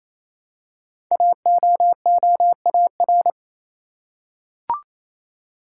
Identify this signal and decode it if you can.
Morse code, heard as a keyed tone, about 27 wpm, 700 Hz. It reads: AOOAR
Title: Code Groups - 5 Letter:Numbers - 28wpm